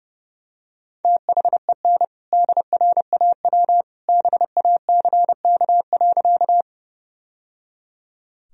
Morse code, heard as a keyed tone, about 30 wpm, 700 Hz. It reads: THEN DRAW BACK.
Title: Sentences from Top 1000 Words - Review Part 1 - 30wpm